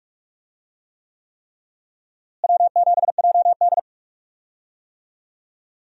Morse code, heard as a keyed tone, about 42 wpm, 700 Hz. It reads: W7JD
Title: USA Call Signs - 4 Characters - Repeated 3x - 45wpm